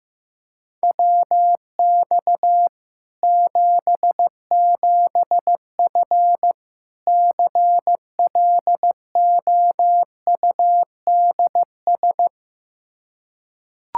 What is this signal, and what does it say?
Morse code, heard as a keyed tone, about 15 wpm, 700 Hz. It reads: WX 77F CLOUDS
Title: Practice QSO - Exchange 2 - Info - 15wpm